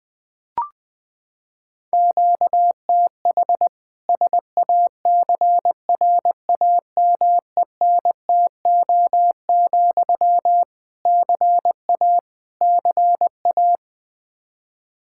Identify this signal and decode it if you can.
Morse code, heard as a keyed tone, about 20 wpm, 700 Hz. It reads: QTH SACRAMENTO, CA CA
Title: Practice QSO - Exchange 1 - Beginning - 20wpm